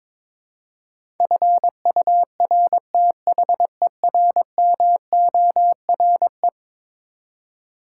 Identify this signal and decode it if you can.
Morse code, heard as a keyed tone, about 22 wpm, 700 Hz. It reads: FURTHERMORE
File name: practice / 22wpm / Eleven Letter Words - 22wpm